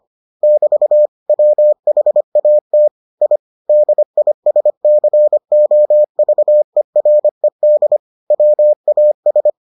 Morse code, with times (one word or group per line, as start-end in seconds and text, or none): 0.4-1.1 <BT>
1.3-2.9 WHAT
3.2-3.4 I
3.7-8.0 DISCOVERED
8.3-9.5 WAS